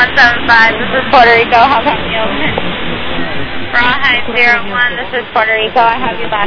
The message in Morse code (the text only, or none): none